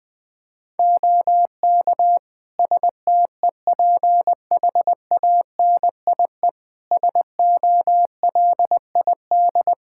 Morse code, heard as a keyed tone, about 20 wpm, 700 Hz. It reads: OK STEPHANIE SOLID
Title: Practice QSO - Exchange 3 - More Info - 20wpm